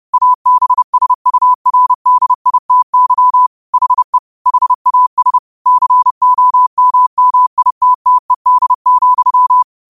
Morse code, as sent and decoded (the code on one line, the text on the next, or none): .- -... ... ..- .-. -.. .. - -.-- / .... / .... .- ... / -.-. --- -- -- .. - - . -.. --..--
ABSURDITY HE HAS COMMITTED,